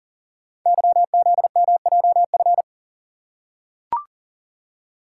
Morse code, heard as a keyed tone, about 38 wpm, 700 Hz. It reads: YZMJF
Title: Code Groups - 5 Letters - 40wpm